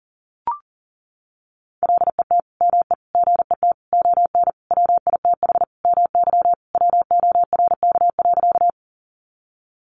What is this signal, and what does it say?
Morse code, heard as a keyed tone, about 40 wpm, 700 Hz. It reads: LET ME GET ON WITH MY WORK.
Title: Sentences from Top 300 Words 40wpm